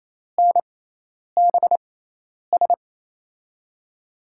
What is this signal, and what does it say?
Morse code, heard as a keyed tone, about 27 wpm, spaced as for 10 wpm, 700 Hz. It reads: NBS
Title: USA Call Signs - 5 Characters - 6x Character Spacing - 28wpm